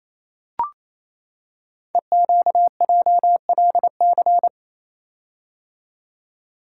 Morse code, heard as a keyed tone, about 29 wpm, 700 Hz. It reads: EQJLC
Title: Code Groups - 5 Letter:Numbers - 28wpm